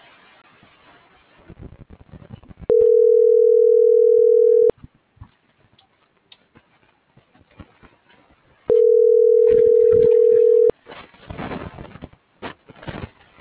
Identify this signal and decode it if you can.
Ringback tone (North America), 440 + 480 Hz.